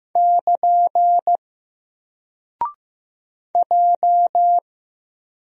Morse code, heard as a keyed tone, about 15 wpm, 700 Hz. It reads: <KN> J
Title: Single Character (Letter:Number:Punctuation:Prosign) 15wpm